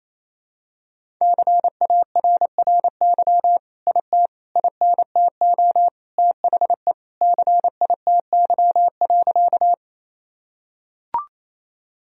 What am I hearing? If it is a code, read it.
Morse code, heard as a keyed tone, about 28 wpm, 700 Hz. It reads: CARRY IT INTO THE CITY.
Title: Sentences from Top 1000 Words - Review Part 1 - 28wpm